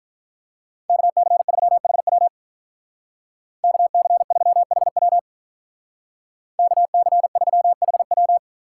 Morse code, heard as a keyed tone, about 47 wpm, 700 Hz. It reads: KC3HW KC3HW KC3HW